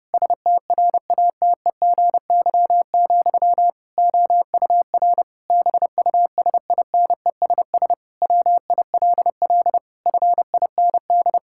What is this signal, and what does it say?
Morse code, heard as a keyed tone, about 30 wpm, 700 Hz. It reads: STRATEGY, OUR BUSINESS WILL FIND
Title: Sentences from Top 1000 Words - Review Part 5 - 30wpm